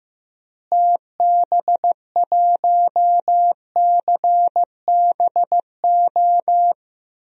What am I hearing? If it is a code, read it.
Morse code, heard as a keyed tone, about 16 wpm, 700 Hz. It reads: TB1CBO
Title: DXCC-100 Call Signs - 6 Characters - Repeated 3x - 15wpm